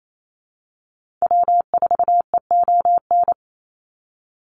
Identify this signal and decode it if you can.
Morse code, heard as a keyed tone, about 28 wpm, 700 Hz. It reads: W4EON
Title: NA SOTA Activator Callsigns 28wpm